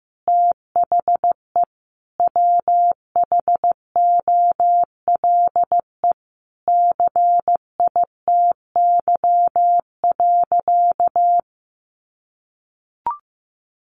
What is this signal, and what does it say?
Morse code, heard as a keyed tone, about 15 wpm, 700 Hz. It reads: THE WHOLE CITY.
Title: Sentences from Top 1400 Words 15wpm